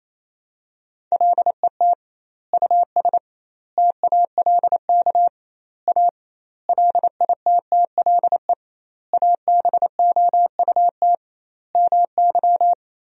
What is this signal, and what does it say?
Morse code, heard as a keyed tone, about 28 wpm, 700 Hz. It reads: LET US TALK A LITTLE ABOUT MY
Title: 2x Word-Spacing - Sentences from Top 400 Words - 28wpm